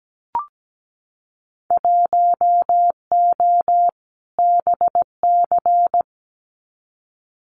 Morse code, heard as a keyed tone, about 17 wpm, 700 Hz. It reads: 1O BC
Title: Winter Field Day Exchanges Encore 17wpm